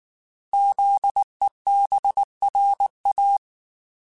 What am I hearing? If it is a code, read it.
Morse code, heard as a keyed tone, about 19 wpm, 800 Hz. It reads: ZEBRA
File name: Alphabetical / Alphabetical mp3s / Zs @18